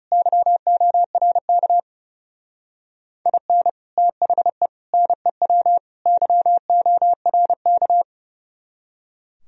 Morse code, heard as a keyed tone, about 38 wpm, 700 Hz. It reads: YORK IN THE NEW YORK
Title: Speed-Racing - Sets of 4 Words 30wpm